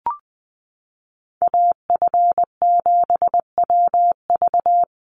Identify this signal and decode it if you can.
Morse code, heard as a keyed tone, about 20 wpm, 700 Hz. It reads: AF7WV
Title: USA Call Signs 20wpm